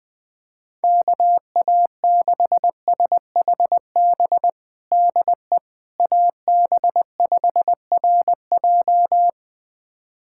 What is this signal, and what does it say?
Morse code, heard as a keyed tone, about 20 wpm, 700 Hz. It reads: KA6SHB DE AB5RJ
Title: USA Callsign DE Callsign 20wpm